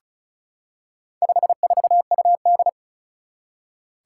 Morse code, heard as a keyed tone, about 36 wpm, 700 Hz. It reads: 54UD
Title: Four Challenging Letter-Number Combinations 35wpm